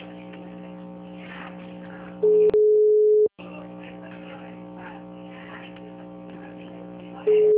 Ringback tone (Europe, 425 Hz).